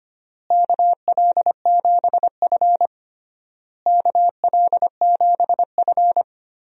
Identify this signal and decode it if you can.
Morse code, heard as a keyed tone, about 25 wpm, 700 Hz. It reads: KL7F KL7F
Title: DXCC-100 Call Signs - 4 Characters - Repeated 3x - 25wpm